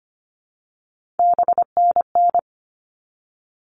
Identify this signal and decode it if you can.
Morse code, heard as a keyed tone, about 25 wpm, 700 Hz. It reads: BNN
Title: All Cut Numbers - 3-Digit 25wpm